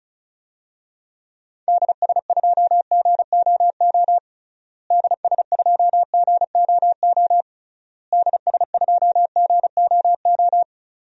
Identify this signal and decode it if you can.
Morse code, heard as a keyed tone, about 34 wpm, 700 Hz. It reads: DS2GOO DS2GOO DS2GOO
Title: DXCC-100 Call Signs - 6 Characters - Repeated 3x - 35wpm